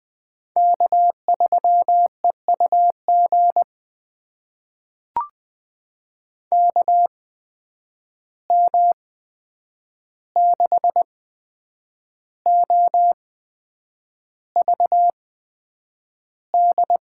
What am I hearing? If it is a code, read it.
Morse code, heard as a keyed tone, about 20 wpm, spaced as for 8 wpm, 700 Hz. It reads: K3EUG KM6OVD
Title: USA Call Signs - 8x Character Spacing - 20wpm